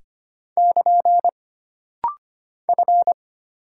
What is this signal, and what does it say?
Morse code, heard as a keyed tone, about 25 wpm, 700 Hz. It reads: <KN> F
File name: Single Character (Letter:Number:Punctuation:Prosign) - Rapid-Fire - 25wpm